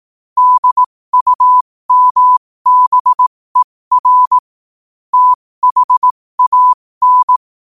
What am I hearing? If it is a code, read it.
Morse code, heard as a keyed tone, about 18 wpm, spaced as for 15 wpm, 1000 Hz. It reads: DUMBER THAN
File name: qotdfiles / 15wpm